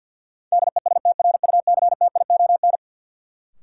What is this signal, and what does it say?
Morse code, heard as a keyed tone, about 50 wpm, 700 Hz. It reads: DESTRUCTION
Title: Sets of 5 Words Encore - New Word Drill - 50wpm